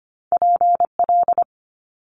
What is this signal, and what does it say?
Morse code, heard as a keyed tone, about 25 wpm, 700 Hz. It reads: PL
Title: Multi-letter Phonemes Quick Fire 25wpm